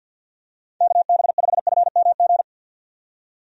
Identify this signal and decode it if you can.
Morse code, heard as a keyed tone, about 47 wpm, 700 Hz. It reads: KB5FMG
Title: USA Call Signs - 6 Characters - 50wpm